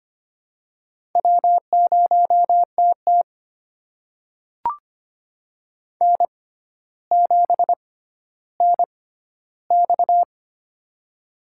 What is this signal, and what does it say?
Morse code, heard as a keyed tone, about 25 wpm, spaced as for 13 wpm, 700 Hz. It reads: W0TT N7NX
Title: USA Call Signs - 4 Characters - 6x Character Spacing - 25wpm